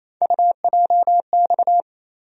Morse code, heard as a keyed tone, about 28 wpm, 700 Hz. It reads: UJX
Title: USA Call Sign Postfixes - 3 Letters 28wpm